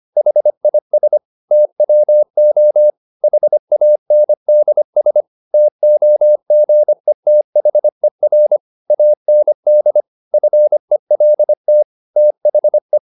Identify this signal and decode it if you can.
Morse code, heard as a keyed tone, about 25 wpm, 600 Hz. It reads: HIS TWO HANDS TOGETHER AND FELT THE